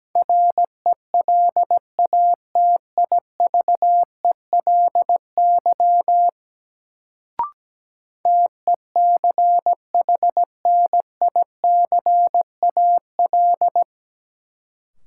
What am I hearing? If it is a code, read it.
Morse code, heard as a keyed tone, about 17 wpm, 700 Hz. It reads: RELATIVELY TECHNICAL
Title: Sentences from Top 1100 Words - New Word Drill - 17wpm